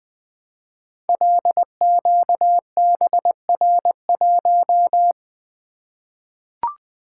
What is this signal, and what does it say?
Morse code, heard as a keyed tone, about 20 wpm, 700 Hz. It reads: LQBR1
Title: Code Groups - 5 Letter:Numbers - 20wpm